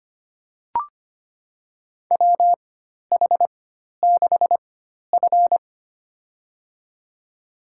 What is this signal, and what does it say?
Morse code, heard as a keyed tone, about 25 wpm, spaced as for 12 wpm, 700 Hz. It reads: WH6F
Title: USA Call Signs - 4x Character Spacing - 25wpm